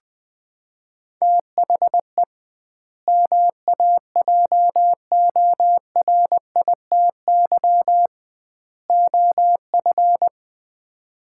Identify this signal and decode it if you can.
Morse code, heard as a keyed tone, about 21 wpm, 700 Hz. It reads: THE MAJORITY OF
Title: 2x Word-Spacing - Sets of 3 Words - 20wpm